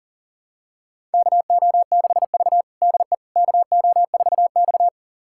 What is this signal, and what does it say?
Morse code, heard as a keyed tone, about 40 wpm, 700 Hz. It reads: KO6V DE KO4X
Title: USA Callsign DE Callsign - 4 Characters - 40wpm